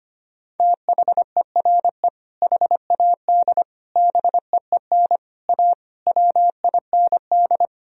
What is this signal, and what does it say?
Morse code, heard as a keyed tone, about 25 wpm, 700 Hz. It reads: THERE HAD BEEN A WIND
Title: Sentences from Top 1000 Words - Review Part 2 - 25wpm